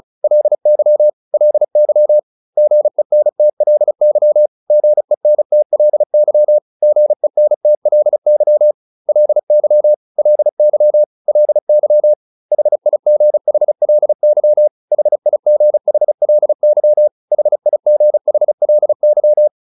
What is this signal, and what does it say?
Morse code, heard as a keyed tone, about 34 wpm, 600 Hz. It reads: LY LY GENTLY GENTLY GENTLY LY LY LY HIGHLY HIGHLY HIGHLY